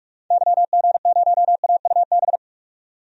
Morse code, heard as a keyed tone, about 42 wpm, 700 Hz. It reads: YG0AUB
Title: DXCC-100 Call Signs - 6 Characters - 45wpm